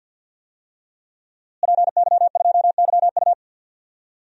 Morse code, heard as a keyed tone, about 52 wpm, 700 Hz. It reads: PY2YU
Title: DXCC-100 Call Signs - 5 Characters - Repeated 3x - 50wpm